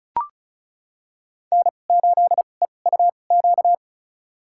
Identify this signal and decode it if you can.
Morse code, heard as a keyed tone, about 36 wpm, spaced as for 25 wpm, 700 Hz. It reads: N8EUQ
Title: USA Call Signs - 5 Characters - 2x Character Spacing - 35wpm